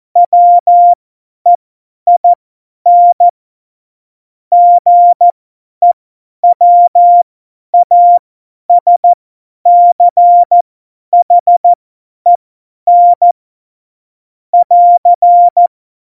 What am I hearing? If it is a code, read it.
Morse code, heard as a keyed tone, about 14 wpm, spaced as for 10 wpm, 700 Hz. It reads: WEIN GEWASCHEN <AR>
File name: Morsekurs / Witze / Witz 09. April 2019, 50 F70